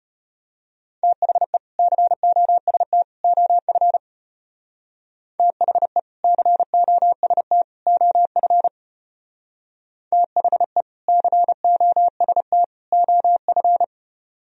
Morse code, heard as a keyed tone, about 31 wpm, 700 Hz. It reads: THE COST OF THE COST OF THE COST OF